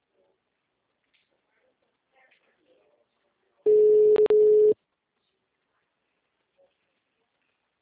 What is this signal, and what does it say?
Ringback tone (Europe, 425 Hz).